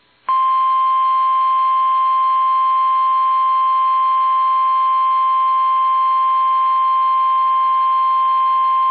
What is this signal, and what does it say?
Emergency Alert System (SAME) tones (the 1050 Hz attention tone). It is heard from 0.8 to 8.5 s.
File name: eas / KUSO / KUSO